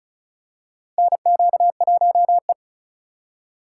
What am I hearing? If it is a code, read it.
Morse code, heard as a keyed tone, about 36 wpm, 690 Hz. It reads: NQ1E
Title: USA Call Signs - 4 Characters - 2x Character Spacing - 35wpm